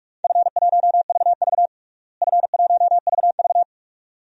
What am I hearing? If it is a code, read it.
Morse code, heard as a keyed tone, about 46 wpm, 700 Hz. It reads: F1VV F1VV
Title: DXCC-100 Call Signs - 4 Characters - Repeated 2x - 45wpm